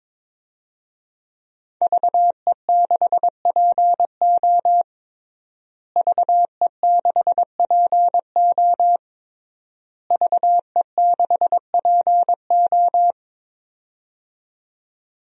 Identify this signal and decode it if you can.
Morse code, heard as a keyed tone, about 22 wpm, 700 Hz. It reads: VE6PO VE6PO VE6PO
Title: DXCC-100 Call Signs - 5 Characters - Repeated 3x - 22wpm